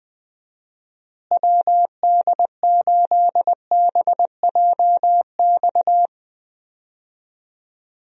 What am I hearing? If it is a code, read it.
Morse code, heard as a keyed tone, about 21 wpm, 700 Hz. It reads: WD8BJX
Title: USA Call Signs - 6 Characters - Repeated 3x - 20wpm